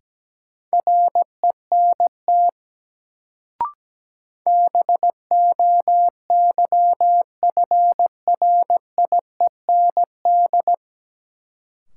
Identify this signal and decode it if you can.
Morse code, heard as a keyed tone, about 17 wpm, 700 Hz. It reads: RENT BOYFRIEND
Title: Sentences from Top 1000 Words - Review Part 5 - New Word Drill - 17wpm